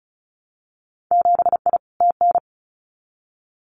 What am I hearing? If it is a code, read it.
Morse code, heard as a keyed tone, about 34 wpm, 700 Hz. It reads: 7I TN
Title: Winter Field Day Exchanges Encore 35wpm